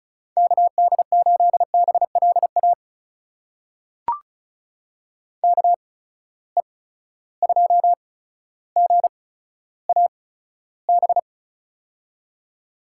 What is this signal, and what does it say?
Morse code, heard as a keyed tone, about 36 wpm, spaced as for 15 wpm, 700 Hz. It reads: KD8BLA KE2GAB
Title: USA Call Signs - 6 Characters - 8x Character Spacing - 35wpm